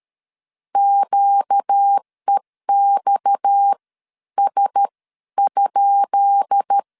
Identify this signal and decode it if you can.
Morse code, heard as a keyed tone, about 13 wpm, spaced as for 10 wpm, 800 Hz. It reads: QEXS?
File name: LEZIONI / 10 SIGNS / 10B INT 12wpm 2414